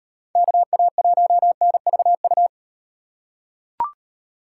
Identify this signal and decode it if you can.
Morse code, heard as a keyed tone, about 38 wpm, 700 Hz. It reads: KA1NVU